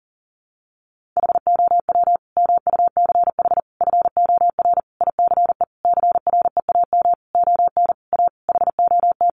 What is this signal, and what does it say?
Morse code, heard as a keyed tone, about 40 wpm, 700 Hz. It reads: HOW MUCH FOR ICE CREAM ON A HOT
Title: Sentences from Top 1500 Words 40wpm